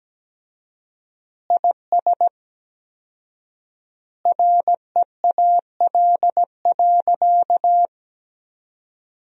Morse code, heard as a keyed tone, about 17 wpm, 700 Hz. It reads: IS REAL.
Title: 4x Word-Spacing - Sentences from Top 300 Words - 17wpm